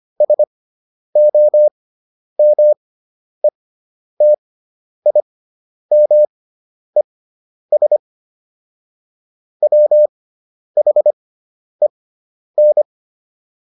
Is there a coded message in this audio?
Yes: Morse code from 0.2 to 12.8 s, heard as a keyed tone, about 25 wpm, spaced as for 10 wpm, 600 Hz. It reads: SOMETIMES WHEN